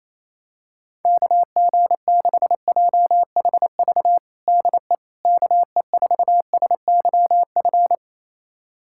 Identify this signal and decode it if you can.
Morse code, heard as a keyed tone, about 28 wpm, 700 Hz. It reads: KG6JHV DE KE4SYF
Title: USA Callsign DE Callsign - 6 Characters - 28wpm